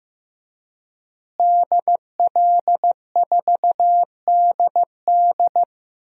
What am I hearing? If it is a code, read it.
Morse code, heard as a keyed tone, about 15 wpm, 700 Hz. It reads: DL4DD
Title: DXCC-100 Call Signs - 5 Characters - Repeated 2x - 15wpm